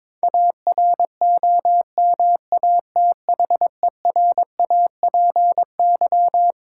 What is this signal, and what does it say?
Morse code, heard as a keyed tone, about 22 wpm, 690 Hz. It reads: AROMATHERAPY
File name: Difficult Words with Hints - 22wpm